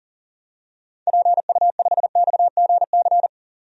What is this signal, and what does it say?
Morse code, heard as a keyed tone, about 40 wpm, 700 Hz. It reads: PU5XGC
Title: DXCC-100 Call Signs - 6 Characters - 40wpm